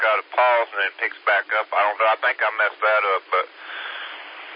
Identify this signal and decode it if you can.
Emergency Alert System (SAME) tones, heard as harsh buzzing bursts, with speech. It is heard from 1.1 to 4.1 s.